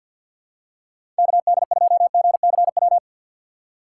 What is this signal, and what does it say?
Morse code, heard as a keyed tone, about 49 wpm, 700 Hz. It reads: KD2GCW
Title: USA Call Signs - 6 Characters - 6x Character Spacing - 50wpm